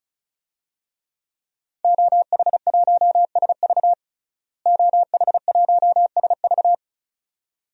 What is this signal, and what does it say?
Morse code, heard as a keyed tone, about 36 wpm, 700 Hz. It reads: OH1SV OH1SV
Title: DXCC-100 Call Signs - 5 Characters - Repeated 2x - 35wpm